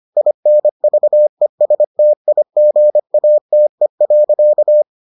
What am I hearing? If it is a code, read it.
Morse code, heard as a keyed tone, about 25 wpm, 600 Hz. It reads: INVESTIGATE.